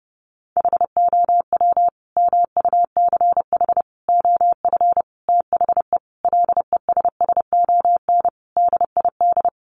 Morse code, heard as a keyed tone, about 30 wpm, 700 Hz. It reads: HOW MUCH OF THE LESSON DID
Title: Sentences from Top 1200 Words 30wpm